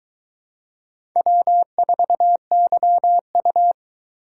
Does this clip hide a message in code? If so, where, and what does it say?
Yes: Morse code from 1.2 to 3.7 s, heard as a keyed tone, about 24 wpm, 700 Hz. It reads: W4YU